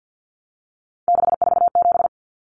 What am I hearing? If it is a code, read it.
Morse code, heard as a keyed tone, about 50 wpm, 700 Hz. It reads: 647
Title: Numbers - 3-Digit 50wpm